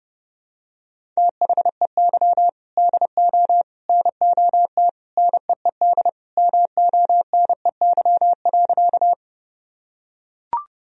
Morse code, heard as a keyed tone, about 30 wpm, 700 Hz. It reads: THEY DO NOT NEED MONEY.
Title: Sentences from Top 1000 Words - Review Part 1 - 30wpm